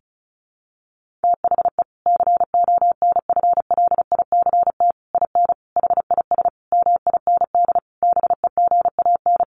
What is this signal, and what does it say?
Morse code, heard as a keyed tone, about 34 wpm, 700 Hz. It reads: THE CONFLICT IN HIS MIND BEGAN
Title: Sentences from Top 1300 Words 35wpm